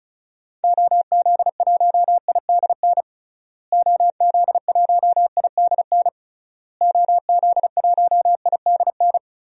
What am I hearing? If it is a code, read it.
Morse code, heard as a keyed tone, about 36 wpm, 700 Hz. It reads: OZ1IDN OZ1IDN OZ1IDN
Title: DXCC-100 Call Signs - 6 Characters - Repeated 3x - 35wpm